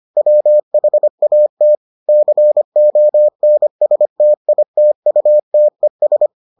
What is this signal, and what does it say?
Morse code, heard as a keyed tone, about 25 wpm, 600 Hz. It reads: WHAT CONSTITUTES